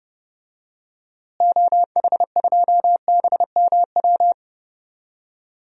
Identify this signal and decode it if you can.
Morse code, heard as a keyed tone, about 30 wpm, 700 Hz. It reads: OH2BMW
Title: DXCC-100 Call Signs - 6 Characters - Repeated 2x - 30wpm